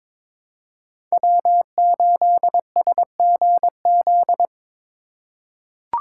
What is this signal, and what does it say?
Morse code, heard as a keyed tone, about 22 wpm, 700 Hz. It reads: W8SGZ
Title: USA Call Signs - 4x Character Spacing - 22wpm